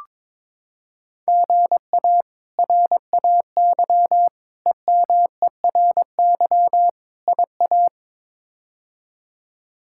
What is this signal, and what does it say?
Morse code, heard as a keyed tone, about 22 wpm, 700 Hz. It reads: GA RAY EMERY IA
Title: K1USN SST QSO Practice - 22wpm